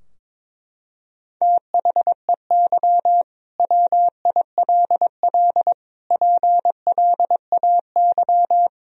Morse code, heard as a keyed tone, about 22 wpm, 700 Hz. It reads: THEY WILL PLAY
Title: Sentences from Top 1000 Words - Review Part 4 - 22wpm